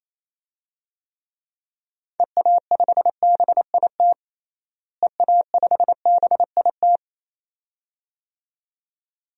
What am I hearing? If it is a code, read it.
Morse code, heard as a keyed tone, about 28 wpm, 700 Hz. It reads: EA5BIT EA5BIT
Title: DXCC-100 Call Signs - 6 Characters - Repeated 2x - 28wpm